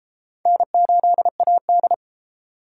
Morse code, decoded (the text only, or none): N8AD